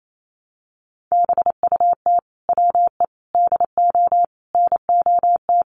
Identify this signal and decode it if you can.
Morse code, heard as a keyed tone, about 28 wpm, 700 Hz. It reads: BUT WE DO NOT
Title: Sets of 5 Words Encore2 28wpm